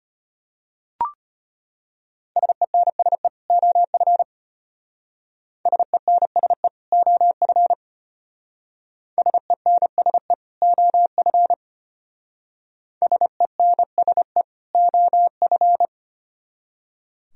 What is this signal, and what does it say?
Morse code, heard as a keyed tone, about 31 wpm, 700 Hz. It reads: SENSE OF SENSE OF SENSE OF SENSE OF
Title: Speed-Racing - Sets of 2 Words 25wpm